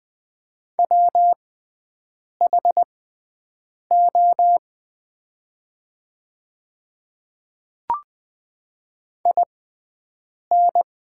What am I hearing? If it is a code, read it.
Morse code, heard as a keyed tone, about 20 wpm, spaced as for 7 wpm, 690 Hz. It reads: WHO IN
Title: 6x Character Spacing - Top 100 Words - 20wpm